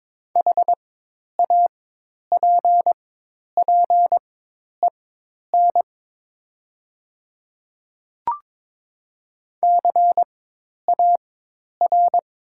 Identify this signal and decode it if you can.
Morse code, heard as a keyed tone, about 22 wpm, spaced as for 10 wpm, 700 Hz. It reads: HAPPEN CAR
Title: 4x Character Spacing - Top 500 Words - 22wpm